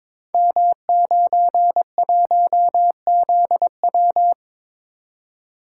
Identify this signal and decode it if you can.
Morse code, heard as a keyed tone, about 22 wpm, 690 Hz. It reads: M91ZW
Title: Code Groups - 5 Letter:Numbers - 22wpm